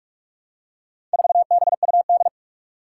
Morse code, heard as a keyed tone, about 46 wpm, 700 Hz. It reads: 4BUD
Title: Four Challenging Letter-Number Combinations 45wpm